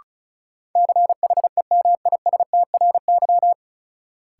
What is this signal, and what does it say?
Morse code, heard as a keyed tone, about 36 wpm, 700 Hz. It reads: CHEMISTRY
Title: Sentences from Top 1000 Words - Review Part 6 - New Word Drill - 35wpm